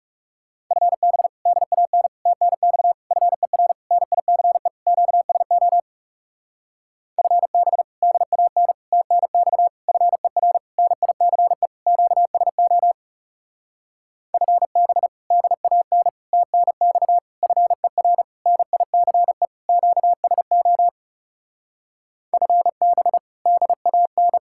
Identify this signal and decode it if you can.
Morse code, heard as a keyed tone, about 36 wpm, 700 Hz. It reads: FB DAN TNX FER NICE QSO FB DAN TNX FER NICE QSO FB DAN TNX FER NICE QSO FB DAN
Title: Speed-Racing Common QSO Phrases - 30wpm